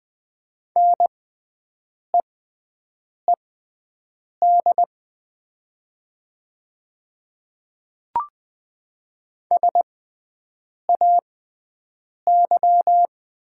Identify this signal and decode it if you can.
Morse code, heard as a keyed tone, about 20 wpm, spaced as for 7 wpm, 700 Hz. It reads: NEED SAY
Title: 6x Character Spacing - Top 300 Words - 20wpm